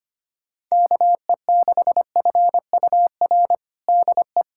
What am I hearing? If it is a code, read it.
Morse code, heard as a keyed tone, about 25 wpm, 700 Hz. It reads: KE6FUR DE
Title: USA Callsign DE Callsign 25wpm